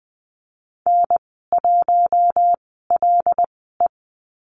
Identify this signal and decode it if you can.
Morse code, heard as a keyed tone, about 20 wpm, spaced as for 14 wpm, 700 Hz. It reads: N1LE
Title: 088 Lesson 15g - Number 5 - Call Signs 20wpm